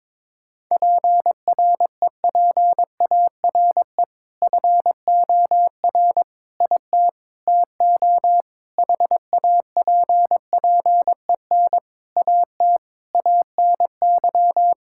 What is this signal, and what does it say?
Morse code, heard as a keyed tone, about 22 wpm, 700 Hz. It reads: PREPARE FOR IT TO HAPPEN AT ANY
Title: Sentences from Top 1000 Words - Review Part 6 - 22wpm